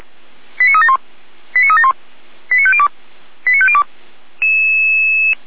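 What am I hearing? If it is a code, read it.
Five-tone selective call (ZVEI), heard as quick runs of five tones, tones of about 70 ms. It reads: 89371 89371 89502 89502
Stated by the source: ZVEI five-tone selective call 89325 received by SDR, from feuerwehr-orlamuende.org (Feuerwehr Orlamünde).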